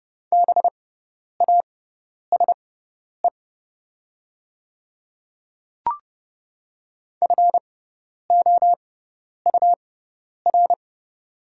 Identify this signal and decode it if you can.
Morse code, heard as a keyed tone, about 30 wpm, spaced as for 10 wpm, 690 Hz. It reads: BASE FOUR
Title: 6x Character Spacing - Top 500 Words - 30wpm